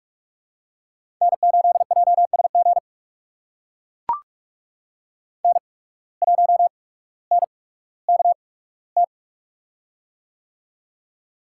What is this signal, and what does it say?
Morse code, heard as a keyed tone, about 44 wpm, spaced as for 19 wpm, 700 Hz. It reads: N8JSG N1NKT
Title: USA Call Signs - 5 Characters - 8x Character Spacing - 45wpm